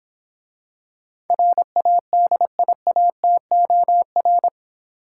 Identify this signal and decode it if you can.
Morse code, heard as a keyed tone, about 26 wpm, 700 Hz. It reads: RADIATOR